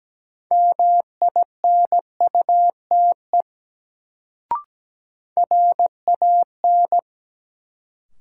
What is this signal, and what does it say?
Morse code, heard as a keyed tone, about 17 wpm, 700 Hz. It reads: MINUTE RAN
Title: Sentences from Top 500 Words - New Word Drill - 17wpm